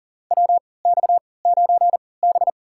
Morse code, heard as a keyed tone, about 40 wpm, spaced as for 23 wpm, 700 Hz. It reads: WX9B
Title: USA Call Signs - 4 Characters - 3x Character Spacing - 40wpm